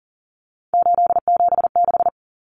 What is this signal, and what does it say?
Morse code, heard as a keyed tone, about 40 wpm, 700 Hz. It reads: 876
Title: Common Cut Numbers - 3-Digit 40wpm